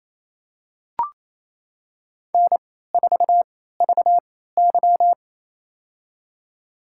Morse code, heard as a keyed tone, about 28 wpm, spaced as for 16 wpm, 700 Hz. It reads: N4VY
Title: USA Call Signs - 4 Characters - 3x Character Spacing - 28wpm